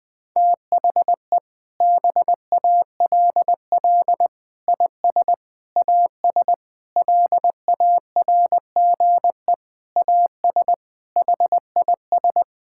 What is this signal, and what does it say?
Morse code, heard as a keyed tone, about 20 wpm, 700 Hz. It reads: THE BALL IS AS LARGE AS HIS
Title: Sentences from Top 1000 Words - Review Part 6 - 20wpm